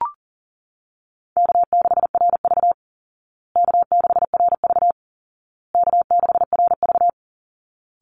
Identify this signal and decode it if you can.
Morse code, heard as a keyed tone, about 40 wpm, 700 Hz. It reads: K6RV K6RV K6RV
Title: Call Signs Repeated 2x - 40wpm